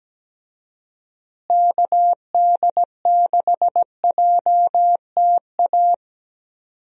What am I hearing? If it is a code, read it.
Morse code, heard as a keyed tone, about 17 wpm, 700 Hz. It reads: KD6JTA
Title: USA Call Signs - 6 Characters - Repeated 3x - 17wpm